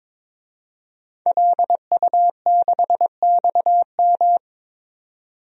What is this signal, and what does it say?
Morse code, heard as a keyed tone, about 22 wpm, 700 Hz. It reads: LU6XM
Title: DXCC-100 Call Signs - 5 Characters - 22wpm